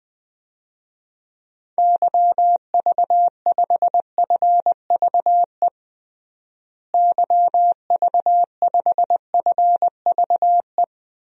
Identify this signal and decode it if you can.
Morse code, heard as a keyed tone, about 20 wpm, 700 Hz. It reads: YV5FVE YV5FVE
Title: DXCC-100 Call Signs - 6 Characters - Repeated 3x - 20wpm